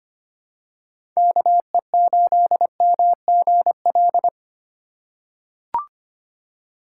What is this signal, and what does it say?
Morse code, heard as a keyed tone, about 25 wpm, 700 Hz. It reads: KE8MGL
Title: USA Call Signs - 6 Characters - 2x Character Spacing - 25wpm